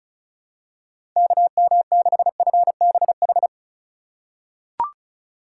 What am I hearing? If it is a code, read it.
Morse code, heard as a keyed tone, about 36 wpm, 700 Hz. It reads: KM6FBH
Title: USA Call Signs - 6 Characters - 2x Character Spacing - 35wpm